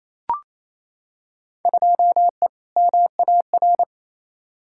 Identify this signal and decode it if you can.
Morse code, heard as a keyed tone, about 28 wpm, 700 Hz. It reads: 2E MAR
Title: Field Day Exchanges Encore - 28wpm